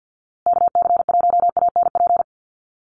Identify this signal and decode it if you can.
Morse code, heard as a keyed tone, about 50 wpm, 700 Hz. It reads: KC1ANP
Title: USA Call Signs Repeated 2x 50wpm